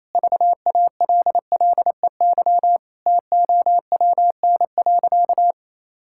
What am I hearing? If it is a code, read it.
Morse code, heard as a keyed tone, about 28 wpm, 700 Hz. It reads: VALLEY TOWN.
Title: Pangram Challenge v2 - 28wpm